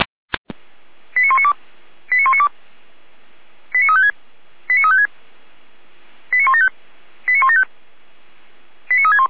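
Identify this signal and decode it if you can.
Five-tone selective call (ZVEI), heard as quick runs of five tones, tones of about 70 ms. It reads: 89192 89192 89356 89356 89175 89175 89361
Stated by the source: ZVEI five-tone call received on an SDR, fire brigade Orlamünde.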